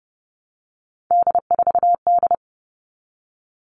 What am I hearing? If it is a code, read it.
Morse code, heard as a keyed tone, about 30 wpm, 700 Hz. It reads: D4D
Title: All Cut Numbers - 3-Digit 30wpm